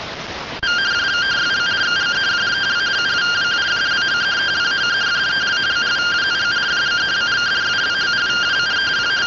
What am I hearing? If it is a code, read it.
Two-tone teleprinter signal, about 170 Hz shift.
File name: findings / axial iodine 3mm 205.gen-06 rtty rx